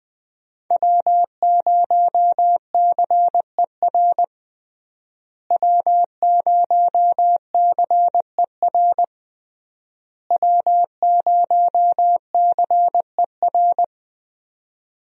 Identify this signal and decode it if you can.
Morse code, heard as a keyed tone, about 21 wpm, 700 Hz. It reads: W0CER W0CER W0CER
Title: USA Call Signs - 5 Characters - Repeated 3x - 20wpm